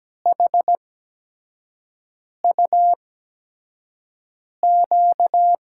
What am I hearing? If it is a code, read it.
Morse code, heard as a keyed tone, about 17 wpm, spaced as for 5 wpm, 700 Hz. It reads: HUQ
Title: USA Call Signs - 6 Characters - 8x Character Spacing - 17wpm